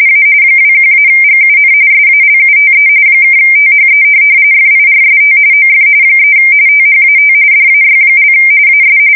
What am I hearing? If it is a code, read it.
RTTY, about 45.45 baud. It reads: A 21-GUN SALUTE (THE FIRST OF 3 THAT DAY) AS
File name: fsk170 / USS-Missouri